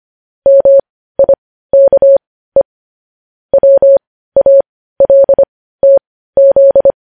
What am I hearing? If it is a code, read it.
Morse code, heard as a keyed tone, about 25 wpm, spaced as for 15 wpm, 550 Hz. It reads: MIKE WALTZ